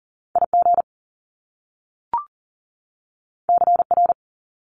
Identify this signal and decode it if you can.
Morse code, heard as a keyed tone, about 42 wpm, 700 Hz. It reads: IG CR
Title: Top Two Letter Combinations in Words 40wpm